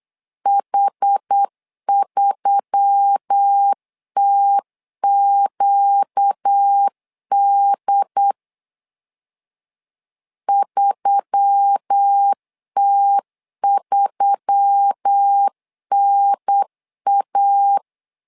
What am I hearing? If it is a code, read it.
Morse code, heard as a keyed tone, about 8 wpm, 800 Hz. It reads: H3TQD 3T3NA
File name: LEZIONI / 4 Q1S3 8wpm 3037